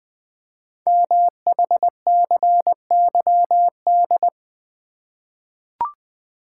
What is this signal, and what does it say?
Morse code, heard as a keyed tone, about 20 wpm, 700 Hz. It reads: MHCYD
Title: Code Groups - 5 Letters - 20wpm